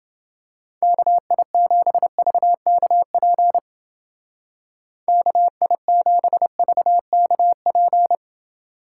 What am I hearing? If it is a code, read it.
Morse code, heard as a keyed tone, about 28 wpm, 700 Hz. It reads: KI7VKP KI7VKP